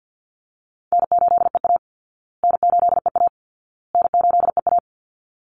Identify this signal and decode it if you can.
Morse code, heard as a keyed tone, about 50 wpm, 700 Hz. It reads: N8EA N8EA N8EA
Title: Call Signs Repeated 2x - 50wpm